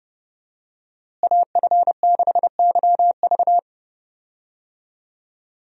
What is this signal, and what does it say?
Morse code, heard as a keyed tone, about 29 wpm, 700 Hz. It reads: AF6YV
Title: USA Call Signs - 5 Characters - Repeated 2x - 30wpm